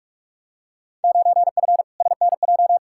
Morse code, heard as a keyed tone, about 42 wpm, 700 Hz. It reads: 9F SNJ
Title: Field Day Exchanges Encore - 45wpm